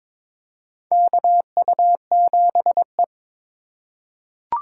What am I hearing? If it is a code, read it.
Morse code, heard as a keyed tone, about 22 wpm, 700 Hz. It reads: KU7E
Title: USA Call Signs - 4x Character Spacing - 22wpm